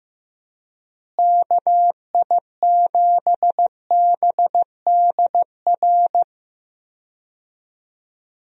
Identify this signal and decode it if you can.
Morse code, heard as a keyed tone, about 15 wpm, 700 Hz. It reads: KI7BDR
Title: USA Call Signs - 6 Characters - Repeated 3x - 15wpm